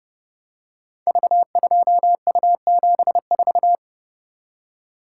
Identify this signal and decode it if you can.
Morse code, heard as a keyed tone, about 29 wpm, 700 Hz. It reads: V2U74
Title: Code Groups - 5 Letter:Numbers - 30wpm